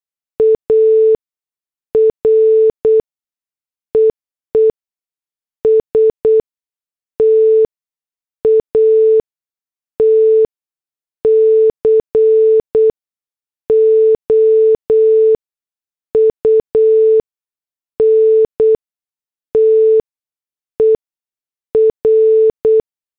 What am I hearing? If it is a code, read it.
Morse code, heard as a keyed tone, about 8 wpm, spaced as for 6 wpm, 440 Hz. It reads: ARE ESTAT COUNTER
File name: morse / headlines